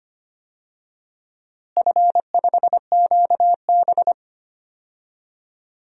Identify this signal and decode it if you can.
Morse code, heard as a keyed tone, about 25 wpm, 700 Hz. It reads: F5QB
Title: DXCC-100 Call Signs - 4 Characters - Repeated 3x - 25wpm